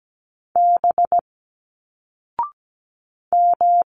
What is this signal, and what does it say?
Morse code, heard as a keyed tone, about 17 wpm, 700 Hz. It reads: B M
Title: Single Letters 17wpm